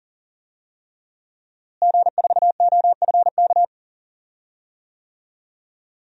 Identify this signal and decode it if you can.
Morse code, heard as a keyed tone, about 40 wpm, 700 Hz. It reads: G4OFK
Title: DXCC-100 Call Signs - 5 Characters - Repeated 3x - 40wpm